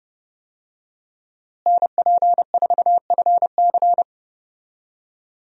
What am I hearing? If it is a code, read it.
Morse code, heard as a keyed tone, about 30 wpm, 700 Hz. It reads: NP4FC
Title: USA Call Signs - 5 Characters - Repeated 3x - 30wpm